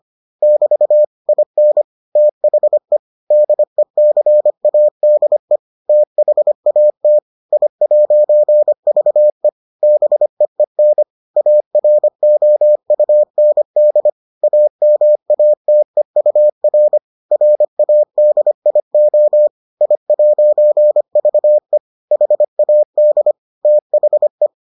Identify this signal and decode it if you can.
Morse code, heard as a keyed tone, about 25 wpm, 600 Hz. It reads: <BT> IN THE DECADE THAT I'VE BEEN AROUND AMATEUR RADIO I'VE HAD THE